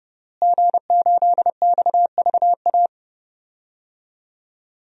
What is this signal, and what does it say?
Morse code, heard as a keyed tone, about 32 wpm, 700 Hz. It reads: G8XVA
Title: DXCC-100 Call Signs - 5 Characters - Repeated 2x - 30wpm